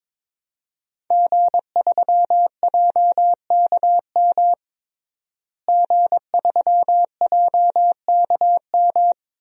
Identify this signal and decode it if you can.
Morse code, heard as a keyed tone, about 21 wpm, 700 Hz. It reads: G3JKM G3JKM
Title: DXCC-100 Call Signs - 5 Characters - Repeated 2x - 22wpm